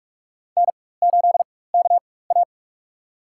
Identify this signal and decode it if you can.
Morse code, heard as a keyed tone, about 43 wpm, spaced as for 21 wpm, 700 Hz. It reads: N8KA
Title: USA Call Signs - 4x Character Spacing - 45wpm